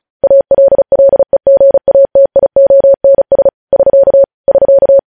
Morse code, heard as a keyed tone, about 36 wpm, 550 Hz. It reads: ALLEGATIONS <SK> <SK>
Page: news headlines in morse code at 35 wpm